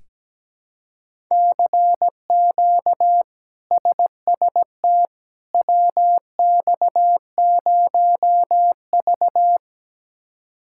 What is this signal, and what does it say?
Morse code, heard as a keyed tone, about 17 wpm, 700 Hz. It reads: CQ SST WX0V
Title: K1USN SST QSO Practice - 17wpm